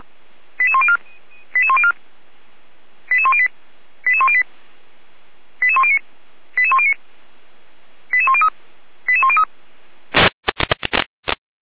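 Five-tone selective call (ZVEI), heard as quick runs of five tones, tones of about 70 ms. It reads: 88104 88104 88108 88108 88109 88109 88103 88103
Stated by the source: ZVEI five-tone selective call received by SDR, feuerwehr-orlamuende.org fire brigade audio archive.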